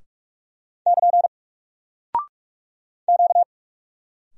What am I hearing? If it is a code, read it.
Morse code, heard as a keyed tone, about 46 wpm, 700 Hz. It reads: <KN> Q
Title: Single Character (Letter:Number:Punctuation:Prosign) 45wpm